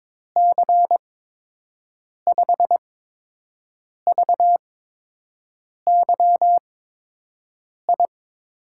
Morse code, heard as a keyed tone, about 21 wpm, spaced as for 6 wpm, 700 Hz. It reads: C5VYI
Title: USA Call Signs - 6 Characters - 8x Character Spacing - 22wpm